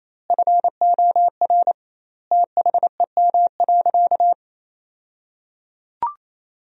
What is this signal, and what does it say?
Morse code, heard as a keyed tone, about 27 wpm, 700 Hz. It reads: FOR THEM.
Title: 2x Word-Spacing - Sentences from Top 300 Words - 28wpm